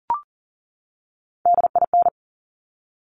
Morse code, heard as a keyed tone, about 40 wpm, 700 Hz. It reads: DIN
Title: Top Three Letter Combinations in Words 40wpm